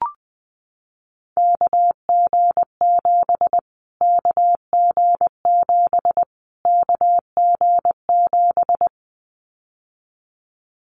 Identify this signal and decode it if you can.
Morse code, heard as a keyed tone, about 20 wpm, 700 Hz. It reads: KG7 KG7 KG7
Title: Callsign Prefixes for Top 25 DXCC - Repeated 3x - 20wpm